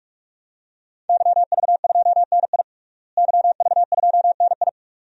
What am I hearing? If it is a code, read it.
Morse code, heard as a keyed tone, about 41 wpm, 700 Hz. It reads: YV2NI YV2NI
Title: DXCC-100 Call Signs - 5 Characters - Repeated 2x - 45wpm